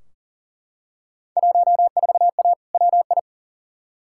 Morse code, heard as a keyed tone, about 40 wpm, 700 Hz. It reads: 14A WI
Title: Field Day Exchanges Encore - 40wpm